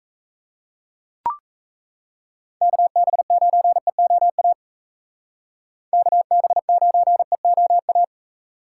Morse code, heard as a keyed tone, about 36 wpm, 700 Hz. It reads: KB9EOA KB9EOA